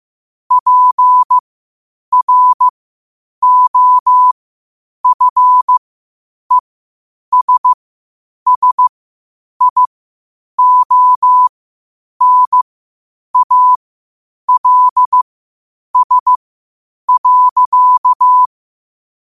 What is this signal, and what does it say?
Morse code, heard as a keyed tone, about 15 wpm, spaced as for 8 wpm, 1000 Hz. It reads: PROFESSIONALS.
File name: qotdfiles / 8wpm